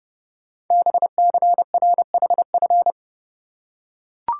Morse code, heard as a keyed tone, about 30 wpm, 700 Hz. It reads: BCRHF
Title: Code Groups - 5 Letters - 30wpm